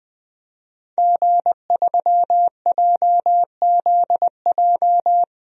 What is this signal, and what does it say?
Morse code, heard as a keyed tone, about 20 wpm, 700 Hz. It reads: G3JZJ
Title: DXCC-100 Call Signs - 5 Characters - 20wpm